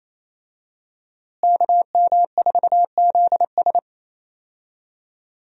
Morse code, heard as a keyed tone, about 28 wpm, 700 Hz. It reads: KM4ZS
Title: USA Call Signs - 5 Characters - Repeated 2x - 28wpm